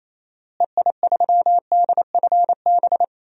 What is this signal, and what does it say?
Morse code, heard as a keyed tone, about 28 wpm, 700 Hz. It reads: EI3DFB
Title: DXCC-100 Call Signs - 6 Characters - Repeated 3x - 28wpm